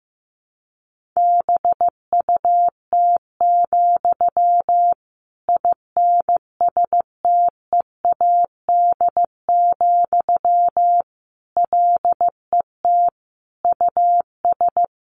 Morse code, heard as a keyed tone, about 15 wpm, 700 Hz. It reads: BUT, INSTEAD, LET US
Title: Sentences from Top 1500 Words 15wpm